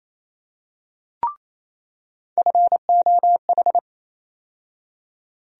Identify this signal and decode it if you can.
Morse code, heard as a keyed tone, about 28 wpm, 700 Hz. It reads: FOH
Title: USA Call Sign Postfixes - 3 Letters 28wpm